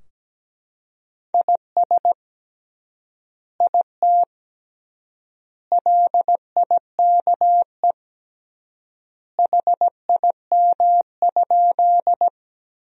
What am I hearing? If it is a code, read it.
Morse code, heard as a keyed tone, about 17 wpm, 700 Hz. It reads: IS IT LIKE HIM?
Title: 3x Word-Spacing - Sentences from Top 100 Words - 17wpm